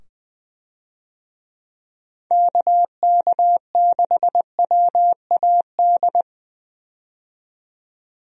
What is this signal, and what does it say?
Morse code, heard as a keyed tone, about 20 wpm, 700 Hz. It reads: KK6WAD
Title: USA Call Signs - 6 Characters - 20wpm